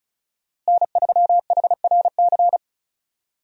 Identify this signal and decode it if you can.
Morse code, heard as a keyed tone, about 36 wpm, 700 Hz. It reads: N3HRC
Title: USA Call Signs - 5 Characters - Repeated 3x - 35wpm